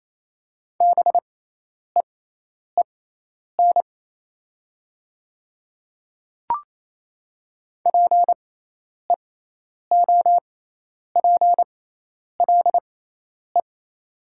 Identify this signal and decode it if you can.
Morse code, heard as a keyed tone, about 28 wpm, spaced as for 10 wpm, 700 Hz. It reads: BEEN PEOPLE
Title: 6x Character Spacing - Top 100 Words - 28wpm